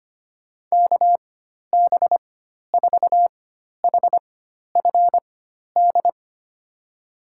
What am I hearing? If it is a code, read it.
Morse code, heard as a keyed tone, about 25 wpm, spaced as for 12 wpm, 700 Hz. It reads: KB4HFD
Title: USA Call Signs - 6 Characters - 4x Character Spacing - 25wpm